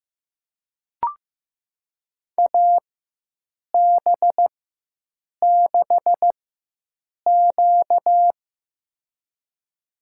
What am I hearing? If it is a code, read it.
Morse code, heard as a keyed tone, about 16 wpm, spaced as for 7 wpm, 700 Hz. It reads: AB6Q
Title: USA Call Signs - 4x Character Spacing - 15wpm